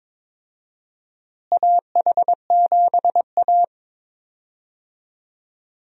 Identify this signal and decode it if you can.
Morse code, heard as a keyed tone, about 22 wpm, 700 Hz. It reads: AH7A
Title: USA Call Signs - 4 Characters - Repeated 2x - 22wpm